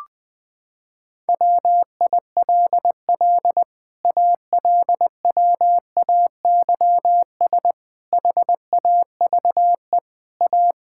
Morse code, heard as a keyed tone, about 20 wpm, 700 Hz. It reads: WILL ALWAYS HAVE A
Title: Sets of 9 Words - 20wpm